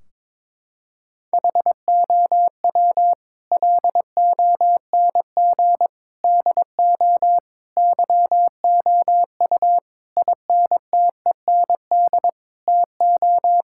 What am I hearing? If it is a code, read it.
Morse code, heard as a keyed tone, about 21 wpm, 700 Hz. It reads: HOW LONG DO YOU INTEND TO
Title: Sentences from Top 1600 Words 22wpm